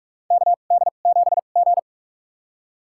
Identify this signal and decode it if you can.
Morse code, heard as a keyed tone, about 46 wpm, spaced as for 33 wpm, 700 Hz. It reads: KD7G
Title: USA Call Signs - 4 Characters - 2x Character Spacing - 45wpm